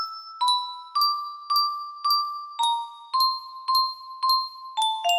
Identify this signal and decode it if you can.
There is no code.